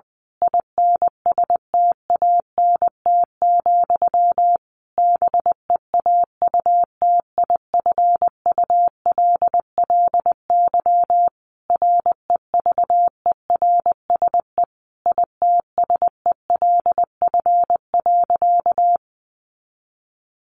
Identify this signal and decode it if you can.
Morse code, heard as a keyed tone, about 20 wpm, 700 Hz. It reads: INSTANT, BEAUTIFULLY REVERSE ITSELF.
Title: A Sound Of Thunder 20wpm